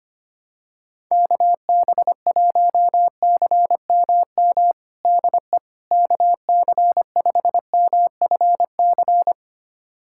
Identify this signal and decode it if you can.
Morse code, heard as a keyed tone, about 25 wpm, 700 Hz. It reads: KB1CMM DE KC5MFC
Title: USA Callsign DE Callsign - 6 Characters - 25wpm